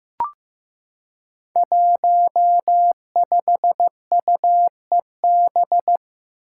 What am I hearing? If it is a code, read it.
Morse code, heard as a keyed tone, about 15 wpm, 700 Hz. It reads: 15UEB
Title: Five Challenging Letter-Number Combinations 15wpm